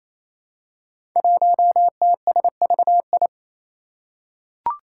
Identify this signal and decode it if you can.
Morse code, heard as a keyed tone, about 27 wpm, 700 Hz. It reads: 1TSVI
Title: Five Challenging Letter-Number Combinations 28wpm